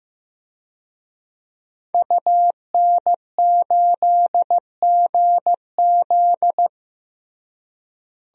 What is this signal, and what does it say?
Morse code, heard as a keyed tone, about 15 wpm, 700 Hz. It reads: UN8GZ
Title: DXCC-100 Call Signs - 5 Characters - Repeated 3x - 15wpm